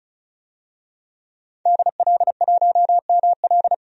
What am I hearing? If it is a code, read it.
Morse code, heard as a keyed tone, about 36 wpm, 700 Hz. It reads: DL1ML
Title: DXCC-100 Call Signs - 5 Characters - 35wpm